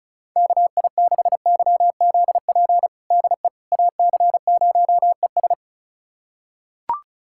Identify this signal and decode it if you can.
Morse code, heard as a keyed tone, about 34 wpm, 700 Hz. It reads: KI6YZP DE AC0ES
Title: USA Callsign DE Callsign 35wpm